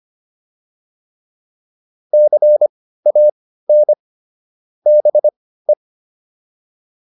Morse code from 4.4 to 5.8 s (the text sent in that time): BE